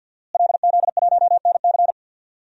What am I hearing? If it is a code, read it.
Morse code, heard as a keyed tone, about 52 wpm, 700 Hz. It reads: LZ1NC
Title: DXCC-100 Call Signs - 5 Characters - Repeated 2x - 50wpm